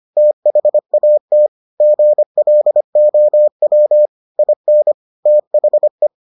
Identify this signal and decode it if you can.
Morse code, heard as a keyed tone, about 25 wpm, 600 Hz. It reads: THAT GLOW IN THE